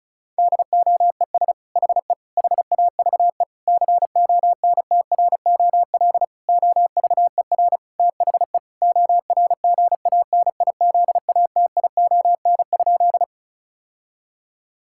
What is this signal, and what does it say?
Morse code, heard as a keyed tone, about 36 wpm, 700 Hz. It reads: DOES HE HAVE CONTROL OVER THE ORGANIZATION?
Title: Sentences from Top 1000 Words - Review Part 5 - 35wpm